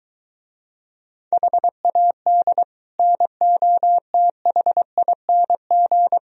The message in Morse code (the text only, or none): HAD NOTHING